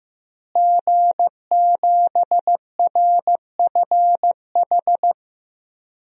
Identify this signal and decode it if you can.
Morse code, heard as a keyed tone, about 15 wpm, 700 Hz. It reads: G7RFH
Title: DXCC-100 Call Signs - 5 Characters - Repeated 2x - 15wpm